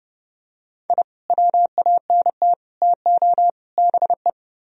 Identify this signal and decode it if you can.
Morse code, heard as a keyed tone, about 29 wpm, 700 Hz. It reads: I WANT TO BE